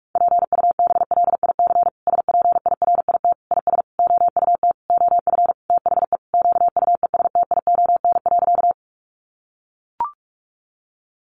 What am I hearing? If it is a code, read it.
Morse code, heard as a keyed tone, about 44 wpm, 700 Hz. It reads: PUBLIC SPIRIT IS OUT OF THE QUESTION.
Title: Sentences from Top 1400 Words 45wpm